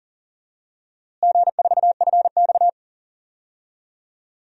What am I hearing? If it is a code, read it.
Morse code, heard as a keyed tone, about 40 wpm, 700 Hz. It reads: G4FX
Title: DXCC-100 Call Signs - 4 Characters - Repeated 3x - 40wpm